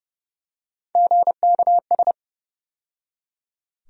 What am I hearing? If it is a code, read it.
Morse code, heard as a keyed tone, about 30 wpm, 700 Hz. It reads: GKS
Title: USA Call Sign Postfixes - 3 Letters 30wpm